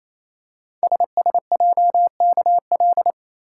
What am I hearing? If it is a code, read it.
Morse code, heard as a keyed tone, about 29 wpm, 700 Hz. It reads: SSJKL
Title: Code Groups - 5 Letters - 28wpm